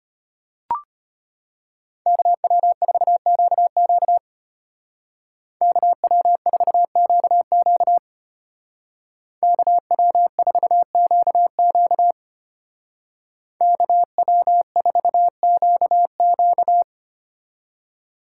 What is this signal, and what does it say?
Morse code, heard as a keyed tone, about 30 wpm, 700 Hz. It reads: KW4QQ KW4QQ KW4QQ KW4QQ
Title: Speed Racing - USA Call Signs 25wpm